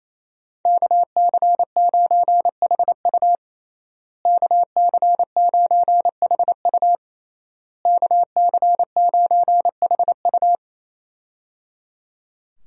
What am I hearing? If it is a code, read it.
Morse code, heard as a keyed tone, about 27 wpm, 700 Hz. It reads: KC9HU KC9HU KC9HU
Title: USA Call Signs - 5 Characters - Repeated 3x - 28wpm